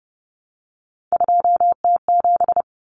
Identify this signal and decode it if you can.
Morse code, heard as a keyed tone, about 30 wpm, 700 Hz. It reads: 2T7
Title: Common Cut Numbers - 3-Digit 30wpm